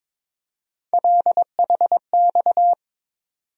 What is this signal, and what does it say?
Morse code, heard as a keyed tone, about 22 wpm, 700 Hz. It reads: LHX
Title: USA Call Sign Postfixes - 3 Letters 22wpm